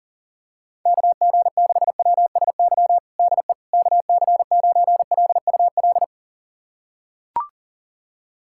Morse code, heard as a keyed tone, about 37 wpm, 700 Hz. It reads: KG6WSY DE KC9LUL
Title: USA Callsign DE Callsign - 6 Characters - 40wpm